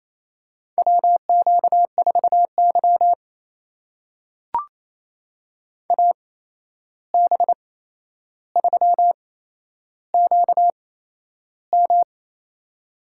Morse code, heard as a keyed tone, about 28 wpm, spaced as for 11 wpm, 700 Hz. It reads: WQ4Y AB3QM
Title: USA Call Signs - 8x Character Spacing - 28wpm